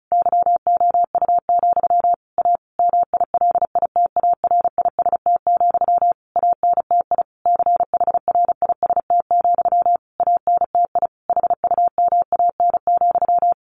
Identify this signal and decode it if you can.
Morse code, heard as a keyed tone, about 34 wpm, 700 Hz. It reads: YOU, A MILITARIST, ANTI CHRIST, ANTI HUMAN,
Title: A Sound Of Thunder 35wpm